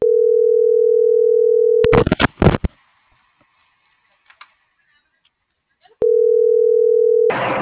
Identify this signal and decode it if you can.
Ringback tone (North America), 440 + 480 Hz.